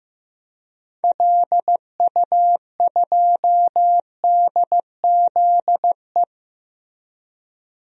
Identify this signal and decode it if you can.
Morse code, heard as a keyed tone, about 15 wpm, 700 Hz. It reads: LU2DZE
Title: DXCC-100 Call Signs - 6 Characters - 15wpm